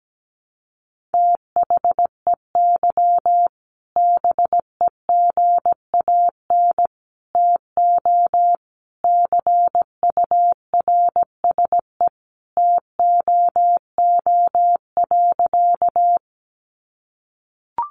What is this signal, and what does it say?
Morse code, heard as a keyed tone, about 17 wpm, 700 Hz. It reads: THEY BEGAN TO CURSE TOO.
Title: A Sound Of Thunder 17wpm